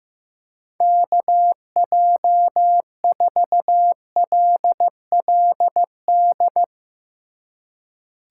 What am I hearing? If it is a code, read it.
Morse code, heard as a keyed tone, about 16 wpm, 700 Hz. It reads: KJ4LLD
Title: Speed Racing - USA Call Signs 15wpm